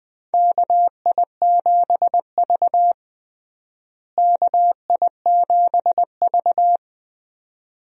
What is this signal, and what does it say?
Morse code, heard as a keyed tone, about 20 wpm, 700 Hz. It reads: KI7V KI7V
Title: USA Call Signs - 4 Characters - Repeated 2x - 20wpm